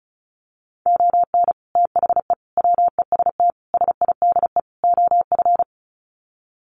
Morse code, heard as a keyed tone, about 34 wpm, 700 Hz. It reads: ON THE WEST SIDE OF
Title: Sets of 5 Words Encore2 35wpm